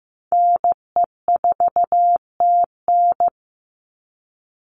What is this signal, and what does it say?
Morse code, heard as a keyed tone, about 15 wpm, 700 Hz. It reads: NE4TN
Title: NA SOTA Chaser Callsigns 15wpm